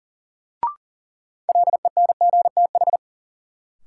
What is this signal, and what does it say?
Morse code, heard as a keyed tone, about 40 wpm, 700 Hz. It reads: LENGTH
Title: Sentences from Top 1000 Words - New Word Drill - 40wpm